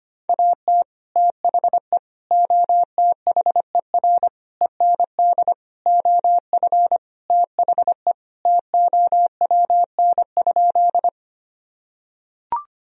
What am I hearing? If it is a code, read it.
Morse code, heard as a keyed tone, about 25 wpm, 700 Hz. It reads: AT THE OTHER END OF THE TOWN?
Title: Sentences from Top 1000 Words - Review Part 2 - 25wpm